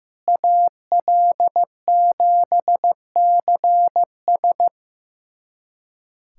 Morse code, heard as a keyed tone, about 15 wpm, 700 Hz. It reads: AL7CS
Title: USA Callsign DE Callsign - 5 Characters - 15wpm